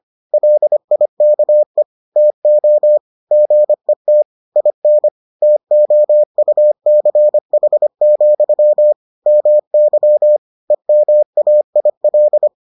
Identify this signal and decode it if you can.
Morse code, heard as a keyed tone, about 25 wpm, 600 Hz. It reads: LIKE TO GET IN TOUCH, MY EMAIL